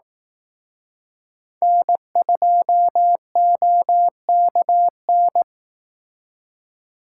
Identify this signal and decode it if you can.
Morse code, heard as a keyed tone, about 18 wpm, 700 Hz. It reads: N2OKN
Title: Speed Racing - USA 1x3 Call Signs 15wpm